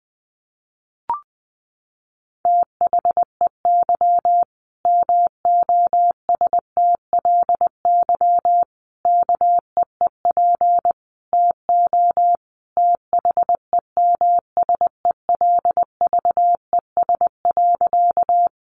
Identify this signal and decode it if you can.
Morse code, heard as a keyed tone, about 20 wpm, 700 Hz. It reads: THEY MOSTLY KEEP TO THEMSELVES.
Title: Sentences from Top 1300 Words 20wpm